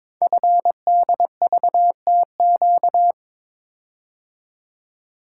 Morse code, heard as a keyed tone, about 22 wpm, 700 Hz. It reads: FDVTQ
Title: Code Groups - 5 Letters - 22wpm